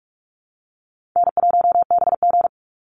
Morse code, heard as a keyed tone, about 44 wpm, 700 Hz. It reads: N1BG
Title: Easy Call Signs - 45wpm